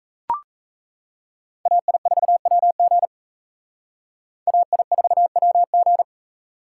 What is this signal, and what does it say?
Morse code, heard as a keyed tone, about 35 wpm, 700 Hz. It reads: AI4WG AI4WG
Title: Speed Racing - USA Call Signs 28wpm